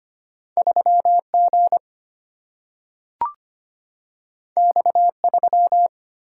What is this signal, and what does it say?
Morse code, heard as a keyed tone, about 25 wpm, 700 Hz. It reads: 3G X3
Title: Single-Letter Single-Number 25wpm